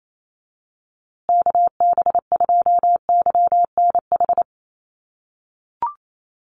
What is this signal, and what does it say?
Morse code, heard as a keyed tone, about 28 wpm, 700 Hz. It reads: KB2YNH
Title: USA Call Signs Repeated 3x 28wpm